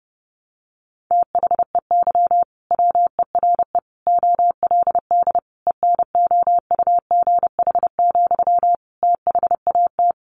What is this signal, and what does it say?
Morse code, heard as a keyed tone, about 30 wpm, 700 Hz. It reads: THEY WERE OLD ENOUGH, THAT
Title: Sentences from Top 800 Words 30wpm